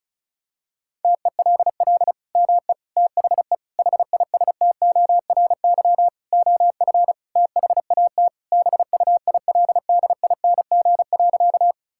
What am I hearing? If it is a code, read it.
Morse code, heard as a keyed tone, about 34 wpm, 700 Hz. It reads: TELL ME THE HISTORY OF THAT BUILDING.
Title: Sentences from Top 1000 Words - Review Part 5 - 35wpm